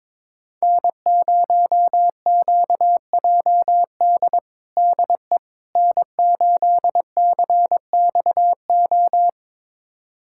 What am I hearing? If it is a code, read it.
Morse code, heard as a keyed tone, about 22 wpm, 700 Hz. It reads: N0QJD DE N8CXO
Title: USA Callsign DE Callsign - 5 Characters - 22wpm